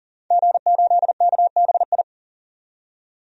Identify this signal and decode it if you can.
Morse code, heard as a keyed tone, about 40 wpm, 700 Hz. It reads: G8KBI
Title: DXCC-100 Call Signs - 5 Characters - 40wpm